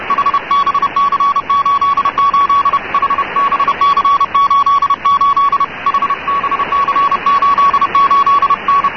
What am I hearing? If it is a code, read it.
Morse code, heard as a keyed tone, about 31 wpm, 1120 Hz, with radio noise. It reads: HBC88 HBC88 HBC88 N
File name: africa / tcd / tcd-55-icrc n'djamena'ttr88'-cw,usb-clg+wkg-hbc88(1985)-(rb)-(mc-909-2-30)